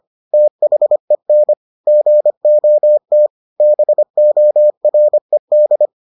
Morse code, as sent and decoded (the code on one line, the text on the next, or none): - .... -. / --. --- - / -... --- .-. -..
THEN GOT BORED